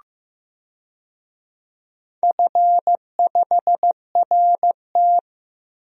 Morse code, heard as a keyed tone, about 15 wpm, 700 Hz. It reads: F5RT
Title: DXCC-100 Call Signs - 4 Characters - 15wpm